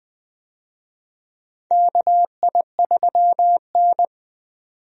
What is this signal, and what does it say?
Morse code, heard as a keyed tone, about 20 wpm, 700 Hz. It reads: KI3N
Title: USA Call Signs - 4 Characters - 20wpm